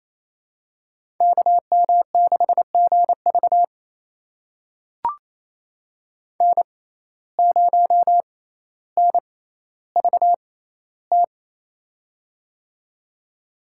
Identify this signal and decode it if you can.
Morse code, heard as a keyed tone, about 28 wpm, spaced as for 14 wpm, 700 Hz. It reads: KM6GV N0NVT
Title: USA Call Signs - 5 Characters - 6x Character Spacing - 28wpm